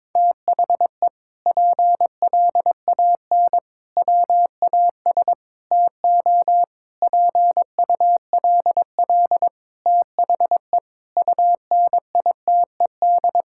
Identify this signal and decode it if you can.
Morse code, heard as a keyed tone, about 22 wpm, 700 Hz. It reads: THE PLAN WAS TO PULL THE UNITED
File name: practice / 22wpm / Sentences with 8-Word Phrases - 22wpm